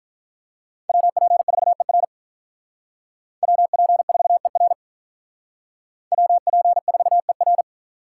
Morse code, heard as a keyed tone, about 46 wpm, 700 Hz. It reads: WP4ER WP4ER WP4ER